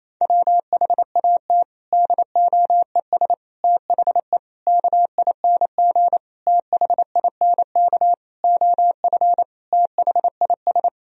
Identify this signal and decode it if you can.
Morse code, heard as a keyed tone, about 28 wpm, 700 Hz. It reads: WHAT DOES THE KING THINK OF THIS
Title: Sentences from Top 1000 Words - Review Part 2 - 28wpm